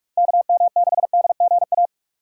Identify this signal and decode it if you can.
Morse code, heard as a keyed tone, about 43 wpm, 700 Hz. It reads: KM6DGA
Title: USA Call Signs - 6 Characters - 4x Character Spacing - 45wpm